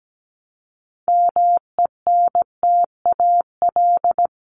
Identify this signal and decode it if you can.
Morse code, heard as a keyed tone, about 17 wpm, 700 Hz. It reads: MENTAL
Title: Top 501-750 Words 17wpm